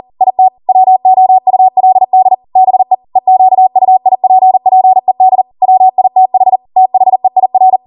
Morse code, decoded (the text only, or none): IT WOULD BE EQUIPPED WITH THEIR